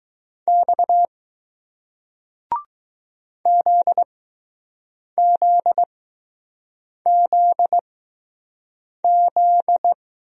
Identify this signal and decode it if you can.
Morse code, heard as a keyed tone, about 20 wpm, spaced as for 6 wpm, 700 Hz. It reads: X ZZZZ